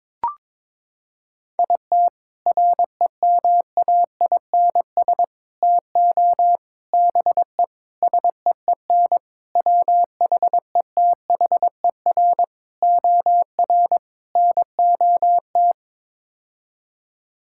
Morse code, heard as a keyed tone, about 22 wpm, 700 Hz. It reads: IT REMAINS TO BE SEEN WHETHER OR NOT
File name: practice / 22wpm / Sets of 8 Words - 22wpm